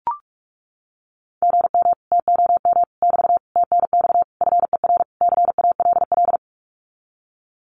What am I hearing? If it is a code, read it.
Morse code, heard as a keyed tone, about 44 wpm, 700 Hz. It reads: GM TOM <BT> TNX FER CALL
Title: Common QSO Phrases Encore 45wpm